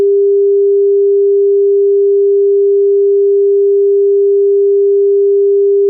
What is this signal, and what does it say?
Dial tone (Japan), 400 Hz.
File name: noise tones dtmf tests / Tones / sine-tone-400hz